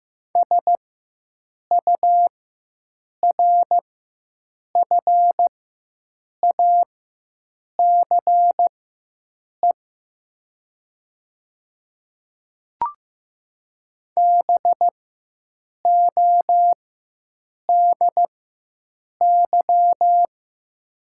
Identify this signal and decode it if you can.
Morse code, heard as a keyed tone, about 15 wpm, spaced as for 7 wpm, 700 Hz. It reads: SURFACE BODY
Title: 4x Character Spacing - Top 500 Words - 15wpm